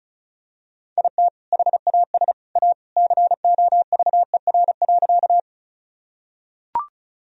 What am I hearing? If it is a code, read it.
Morse code, heard as a keyed tone, about 36 wpm, 700 Hz. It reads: IT HAS A COVER.
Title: Sentences from Top 1000 Words - Review Part 1 - 35wpm